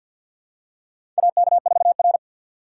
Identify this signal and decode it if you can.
Morse code, heard as a keyed tone, about 47 wpm, 700 Hz. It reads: AK4R
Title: USA Call Signs - 4 Characters - 3x Character Spacing - 50wpm